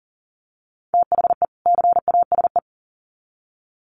Morse code, heard as a keyed tone, about 40 wpm, 700 Hz. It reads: THE CASE
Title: Sets of 2 Words 40wpm